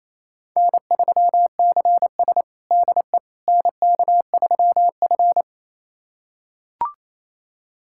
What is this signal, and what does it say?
Morse code, heard as a keyed tone, about 28 wpm, 700 Hz. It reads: N3CS DE NK3F
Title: USA Callsign DE Callsign - 4 Characters - 28wpm